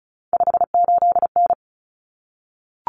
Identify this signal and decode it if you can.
Morse code, heard as a keyed tone, about 34 wpm, 700 Hz. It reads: 58N
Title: Common Cut Numbers - 3-Digit 35wpm